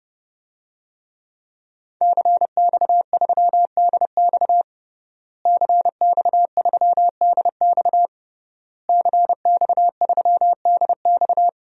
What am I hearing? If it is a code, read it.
Morse code, heard as a keyed tone, about 30 wpm, 700 Hz. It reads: CX3DX CX3DX CX3DX
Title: DXCC-100 Call Signs - 5 Characters - Repeated 3x - 30wpm